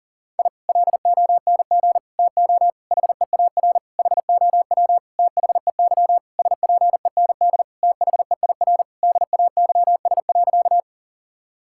Morse code, heard as a keyed tone, about 40 wpm, 700 Hz. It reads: I LONG TO HEAR HOW THEY SPEND THEIR DAYS.
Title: Sentences from Top 1000 Words - Review Part 5 - 40wpm